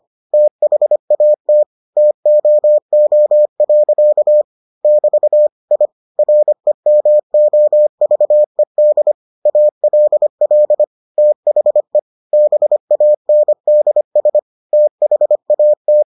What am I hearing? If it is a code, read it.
Morse code, heard as a keyed tone, about 25 wpm, 600 Hz. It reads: THAT TOO. <BT> I REMOVED ALL THE BANDS THAT